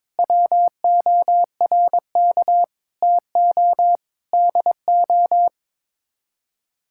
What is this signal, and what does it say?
Morse code, heard as a keyed tone, about 22 wpm, 700 Hz. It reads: WORK TO DO
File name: Sets of 9 Words - 22wpm